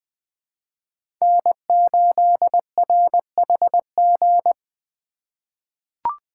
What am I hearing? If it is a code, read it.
Morse code, heard as a keyed tone, about 20 wpm, 700 Hz. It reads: N8RHG
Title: USA Call Signs - 5 Characters - 4x Character Spacing - 20wpm